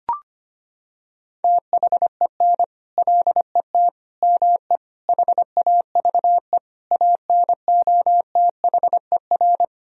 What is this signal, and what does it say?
Morse code, heard as a keyed tone, about 25 wpm, 700 Hz. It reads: THEN LET ME HAVE ANOTHER
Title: Sentences from Top 1000 Words - Review Part 2 - 25wpm